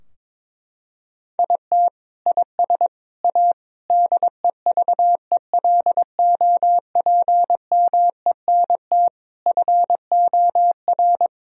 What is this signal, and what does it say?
Morse code, heard as a keyed tone, about 22 wpm, 700 Hz. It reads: IT IS A DEVELOPMENT FOR
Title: Sentences from Top 1000 Words - Review Part 4 - 22wpm